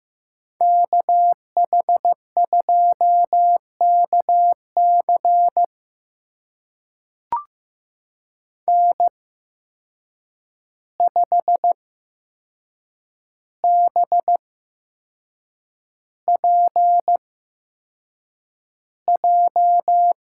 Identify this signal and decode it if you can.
Morse code, heard as a keyed tone, about 15 wpm, spaced as for 6 wpm, 700 Hz. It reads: KH2KC N5BPJ
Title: USA Call Signs - 5 Characters - 8x Character Spacing - 15wpm